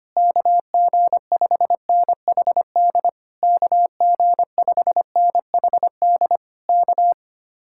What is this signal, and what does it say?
Morse code, heard as a keyed tone, about 24 wpm, 700 Hz. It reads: KG5NHD KG5NHD K
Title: Speed-Racing Common QSO Phrases Encore - 25wpm